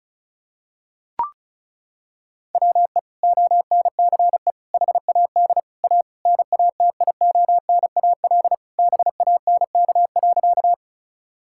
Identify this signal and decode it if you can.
Morse code, heard as a keyed tone, about 36 wpm, 700 Hz. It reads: WE ONCE HAD A NATIONAL BANK.
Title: Sentences from Top 1000 Words - Review Part 6 - 35wpm